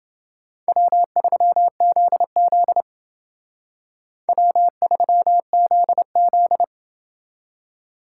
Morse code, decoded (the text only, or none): W3ZZ W3ZZ